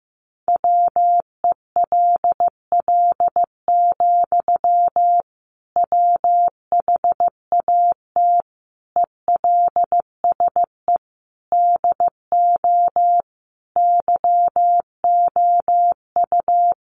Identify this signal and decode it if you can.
Morse code, heard as a keyed tone, about 15 wpm, 700 Hz. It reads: WELL, WHAT ELSE DO YOU
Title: Sentences from Top 800 Words 15wpm